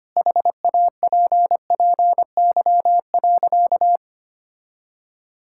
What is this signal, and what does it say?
Morse code, heard as a keyed tone, about 25 wpm, 700 Hz. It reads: HAPPY.
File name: practice / 25wpm / Sentences with 2-Word Phrases - 25wpm